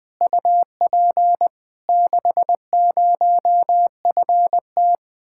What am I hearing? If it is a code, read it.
Morse code, heard as a keyed tone, about 20 wpm, 700 Hz. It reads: UP 60FT
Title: Speed-Racing Common QSO Phrases - 17wpm